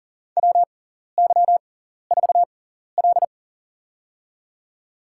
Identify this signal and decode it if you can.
Morse code, heard as a keyed tone, about 40 wpm, spaced as for 14 wpm, 700 Hz. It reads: WY4L
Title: USA Call Signs - 6x Character Spacing - 40wpm